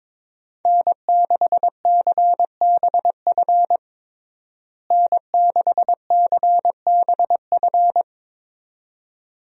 Morse code, heard as a keyed tone, about 22 wpm, 700 Hz. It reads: N6CBF N6CBF
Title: USA Call Signs - 5 Characters - Repeated 2x - 22wpm